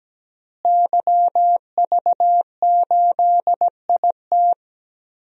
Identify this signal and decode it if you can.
Morse code, heard as a keyed tone, about 17 wpm, 700 Hz. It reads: YV8IT
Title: DXCC-100 Call Signs - 5 Characters - Repeated 3x - 17wpm